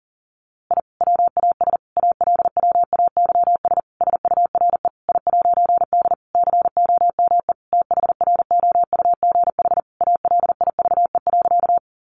Morse code, heard as a keyed tone, about 40 wpm, 700 Hz. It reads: I WAS ALWAYS SURE I'D COME THROUGH ALIVE.
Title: A Sound Of Thunder 40wpm